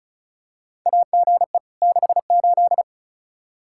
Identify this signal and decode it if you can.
Morse code, heard as a keyed tone, about 34 wpm, 700 Hz. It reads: AGE 68
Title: Practice QSO - Exchange 3 - More Info - 35wpm